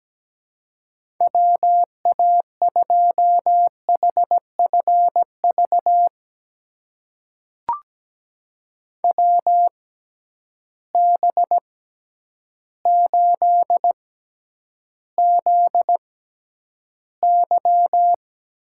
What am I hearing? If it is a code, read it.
Morse code, heard as a keyed tone, about 17 wpm, spaced as for 9 wpm, 700 Hz. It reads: WA2HFV WB8ZY
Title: USA Call Signs - 6 Characters - 6x Character Spacing - 17wpm